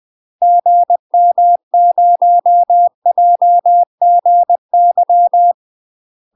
Morse code, heard as a keyed tone, about 20 wpm, 700 Hz. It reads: GM0JGY
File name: UK MASTER QSOs TRAINING / 20WPM / UK MASTER QSOs 0058